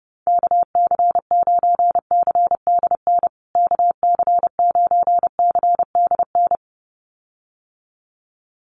Morse code, heard as a keyed tone, about 30 wpm, 700 Hz. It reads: KC9CDN KC9CDN
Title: USA Call Signs Repeated 2x 30wpm